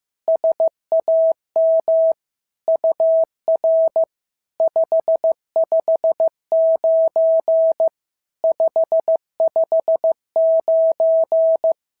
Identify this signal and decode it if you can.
Morse code, heard as a keyed tone, about 15 wpm, 650 Hz. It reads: SAM UR 559 559
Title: SOTA QSO Practice 15wpm